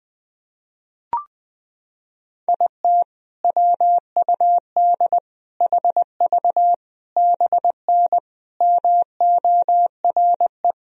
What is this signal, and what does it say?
Morse code, heard as a keyed tone, about 20 wpm, 700 Hz. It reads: IT WUD HV BN MORE
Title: Sentences from Top 400 Words and Abbreviations 20wpm